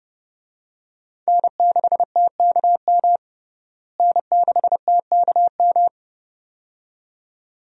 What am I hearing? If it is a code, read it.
Morse code, heard as a keyed tone, about 30 wpm, 700 Hz. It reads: N6TKM N6TKM
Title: USA Call Signs - 5 Characters - Repeated 2x - 30wpm